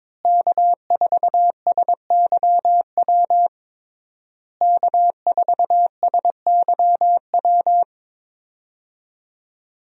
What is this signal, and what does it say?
Morse code, heard as a keyed tone, about 22 wpm, 700 Hz. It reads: K4SYW K4SYW
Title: USA Call Signs - 5 Characters - Repeated 3x - 22wpm